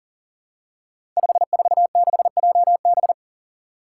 Morse code, heard as a keyed tone, about 38 wpm, 700 Hz. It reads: 546JB
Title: Five Challenging Letter-Number Combinations 40wpm